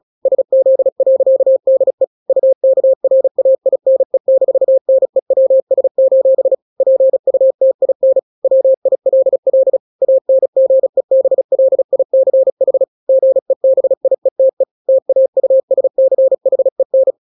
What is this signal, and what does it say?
Morse code, heard as a keyed tone, about 36 wpm, 530 Hz. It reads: SZ.DE UKRAINE-NEWS: PUTIN WILL ANGEBLICH GEBIETE TAUSCHEN
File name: daily / 35wpm